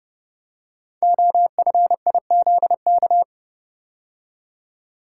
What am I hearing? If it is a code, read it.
Morse code, heard as a keyed tone, about 30 wpm, 700 Hz. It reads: OFIZK
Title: Code Groups - 5 Letters - 30wpm